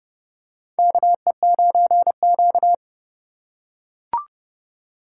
Morse code, heard as a keyed tone, about 30 wpm, 700 Hz. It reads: KE9Q
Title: USA Call Signs - 4 Characters - 3x Character Spacing - 30wpm